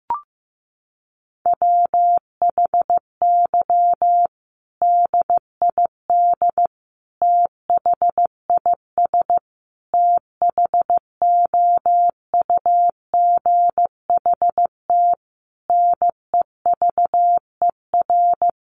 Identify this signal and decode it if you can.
Morse code, heard as a keyed tone, about 14 wpm, 700 Hz. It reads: WHY DID THIS THOUGHT NEVER
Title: Sentences from Top 900 Words 15wpm